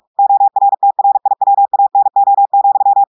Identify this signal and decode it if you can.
Morse code, heard as a keyed tone, about 46 wpm, 800 Hz. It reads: ORTRIWANO,
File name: portugais texte en clair 1 vitesse 45 mots minutes